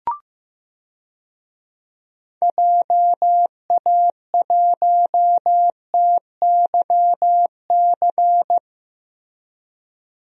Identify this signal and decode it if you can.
Morse code, heard as a keyed tone, about 15 wpm, 700 Hz. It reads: JA1TYC
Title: DXCC-100 Call Signs - 6 Characters - Repeated 2x - 15wpm